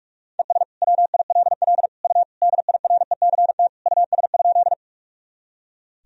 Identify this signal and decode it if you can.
Morse code, heard as a keyed tone, about 46 wpm, 700 Hz. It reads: ES WILL U DIRECT US?
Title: Sentences from Top 400 Words and Abbreviations 45wpm